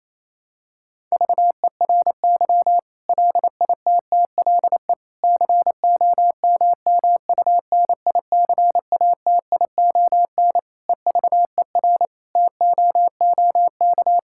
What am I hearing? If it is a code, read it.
Morse code, heard as a keyed tone, about 28 wpm, 700 Hz. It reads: VERY LITTLE COMMUNICATION EVER TOOK
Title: Sentences from Top 1000 Words - Review Part 6 - 28wpm